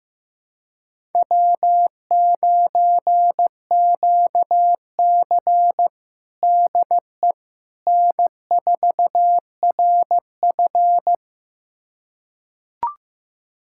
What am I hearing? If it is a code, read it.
Morse code, heard as a keyed tone, about 15 wpm, 700 Hz. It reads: W9QC DE N4RF
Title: USA Callsign DE Callsign - 4 Characters - 15wpm